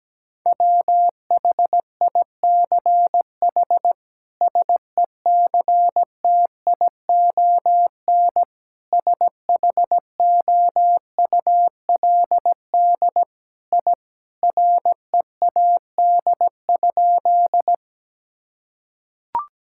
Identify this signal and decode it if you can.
Morse code, heard as a keyed tone, about 17 wpm, 700 Hz. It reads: WHICH SECTION SHOULD I READ?
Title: Sentences from Top 1000 Words - Review Part 5 - 17wpm